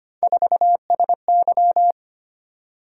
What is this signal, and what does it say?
Morse code, heard as a keyed tone, about 25 wpm, 700 Hz. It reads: 4SY
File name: Three Letter-Number Combinations - 25wpm